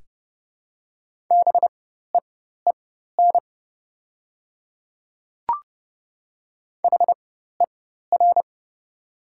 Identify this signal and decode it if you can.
Morse code, heard as a keyed tone, about 30 wpm, spaced as for 14 wpm, 700 Hz. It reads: BEEN HER
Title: 4x Character Spacing - Top 100 Words - 30wpm